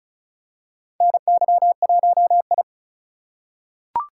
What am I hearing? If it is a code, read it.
Morse code, heard as a keyed tone, about 36 wpm, 700 Hz. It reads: NY1I
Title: USA Call Signs - 4 Characters - 2x Character Spacing - 35wpm